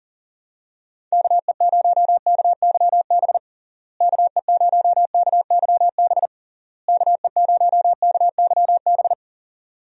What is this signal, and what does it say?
Morse code, heard as a keyed tone, about 40 wpm, 700 Hz. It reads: KE0KYB KE0KYB KE0KYB
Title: USA Call Signs - 6 Characters - Repeated 3x - 40wpm